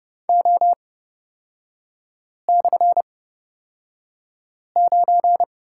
Morse code, heard as a keyed tone, about 30 wpm, spaced as for 5 wpm, 700 Hz. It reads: O/9
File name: Lesson 36e - BT - Three Characters 30wpm